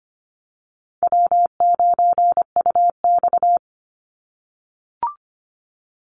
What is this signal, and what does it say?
Morse code, heard as a keyed tone, about 25 wpm, 700 Hz. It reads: W9UX
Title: NA SOTA Chaser Callsigns 25wpm